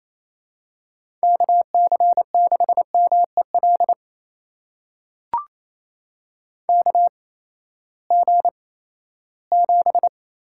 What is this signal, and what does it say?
Morse code, heard as a keyed tone, about 27 wpm, spaced as for 16 wpm, 700 Hz. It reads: KC6MEL KG7
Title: USA Call Signs - 6 Characters - 8x Character Spacing - 28wpm